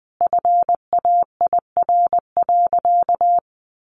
Morse code, heard as a keyed tone, about 20 wpm, 700 Hz. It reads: FAIR.
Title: Sentences from Top 1300 Words 20wpm